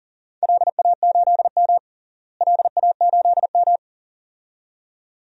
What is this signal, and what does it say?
Morse code, heard as a keyed tone, about 40 wpm, 700 Hz. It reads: LA8M LA8M
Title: DXCC-100 Call Signs - 4 Characters - Repeated 2x - 40wpm